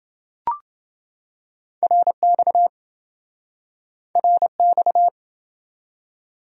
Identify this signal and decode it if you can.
Morse code, heard as a keyed tone, about 26 wpm, 700 Hz. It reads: RX RX